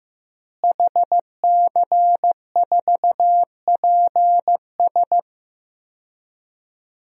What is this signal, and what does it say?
Morse code, heard as a keyed tone, about 15 wpm, 700 Hz. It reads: HC4PS
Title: DXCC-100 Call Signs - 5 Characters - Repeated 3x - 15wpm